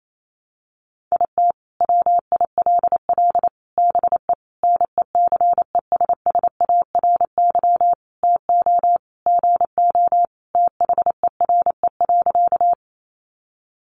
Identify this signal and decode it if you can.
Morse code, heard as a keyed tone, about 28 wpm, 700 Hz. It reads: IT WILL BE NECESSARY TO GO THERE.
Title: Sentences from Top 800 Words 28wpm